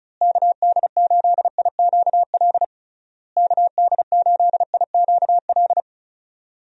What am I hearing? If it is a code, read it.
Morse code, heard as a keyed tone, about 34 wpm, 700 Hz. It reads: KD8IQL KD8IQL
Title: USA Call Signs - 6 Characters - Repeated 3x - 35wpm